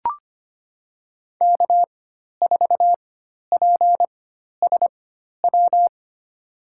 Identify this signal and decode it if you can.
Morse code, heard as a keyed tone, about 25 wpm, spaced as for 12 wpm, 700 Hz. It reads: K4PSW
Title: USA Call Signs - 5 Characters - 4x Character Spacing - 25wpm